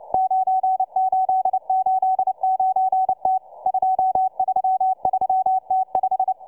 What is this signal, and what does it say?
Morse code, heard as a keyed tone, about 29 wpm, 760 Hz, with radio noise. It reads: 9889T 233T5